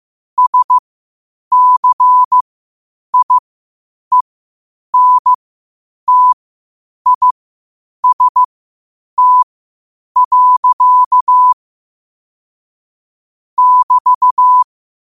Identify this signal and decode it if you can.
Morse code, heard as a keyed tone, about 15 wpm, spaced as for 8 wpm, 1000 Hz. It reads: SCIENTIST. <BT>
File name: qotdfiles / 8wpm